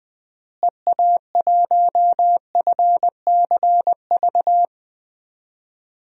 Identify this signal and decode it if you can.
Morse code, heard as a keyed tone, about 21 wpm, 700 Hz. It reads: EA1FCV
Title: DXCC-100 Call Signs - 6 Characters - Repeated 2x - 20wpm